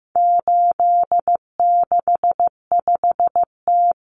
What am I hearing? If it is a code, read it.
Morse code, heard as a keyed tone, about 15 wpm, 700 Hz. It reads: 865T
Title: Common Cut Numbers - 4-Digit 15wpm